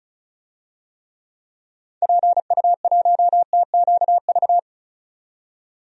Morse code, heard as a keyed tone, about 36 wpm, 700 Hz. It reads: PU1TQV
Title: DXCC-100 Call Signs - 6 Characters - 35wpm